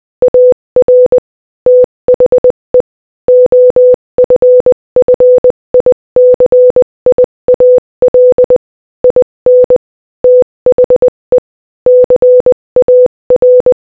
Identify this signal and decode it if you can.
Morse code, heard as a keyed tone, about 20 wpm, 500 Hz. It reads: AR THE OFFICIAL IN THE CAR
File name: Morse-Code-Audio-Speed-20-6